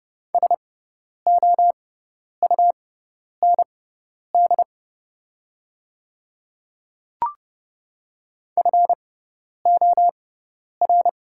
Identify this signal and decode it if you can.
Morse code, heard as a keyed tone, about 30 wpm, spaced as for 10 wpm, 700 Hz. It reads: SOUND FOR
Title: 6x Character Spacing - Top 100 Words - 30wpm